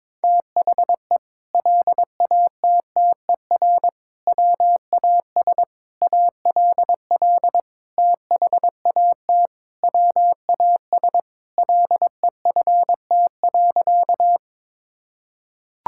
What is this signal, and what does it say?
Morse code, heard as a keyed tone, about 22 wpm, 700 Hz. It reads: THE LATTER WAS ALL THAT WAS LEFT.
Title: Sentences from Top 1600 Words 22wpm